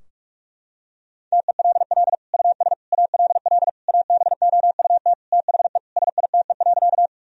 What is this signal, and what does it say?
Morse code, heard as a keyed tone, about 46 wpm, 700 Hz. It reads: TELL US ALL ABOUT THE SITE.
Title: Sentences from Top 1000 Words - Review Part 4 - 45wpm